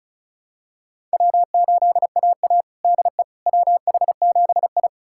Morse code, heard as a keyed tone, about 36 wpm, 700 Hz. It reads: W8AA DE WH7I
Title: USA Callsign DE Callsign - 4 Characters - 35wpm